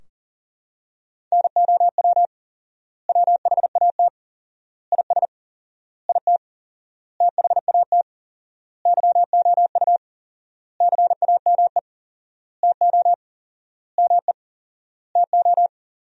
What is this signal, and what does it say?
Morse code, heard as a keyed tone, about 40 wpm, 700 Hz. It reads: NOW WHAT IS IT THAT YOU CAME TO ME TO
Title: 4x Word-Spacing - Sentences from Top 200 Words - 40wpm